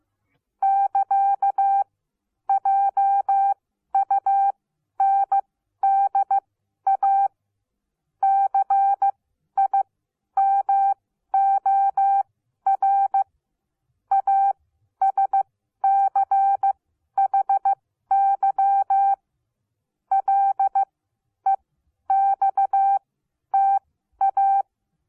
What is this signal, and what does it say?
Morse code, heard as a keyed tone, about 15 wpm, spaced as for 11 wpm, 780 Hz. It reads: <KA> JUNDA CIMOR ASCHY LEXTA
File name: Onlinelehrgang / morsen / DJ4UF Morsen14